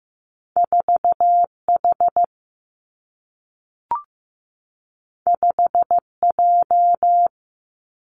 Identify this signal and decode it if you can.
Morse code, heard as a keyed tone, about 15 wpm, 700 Hz. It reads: 4H 5J
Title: Two Challenging Letter-Number Combinations 15wpm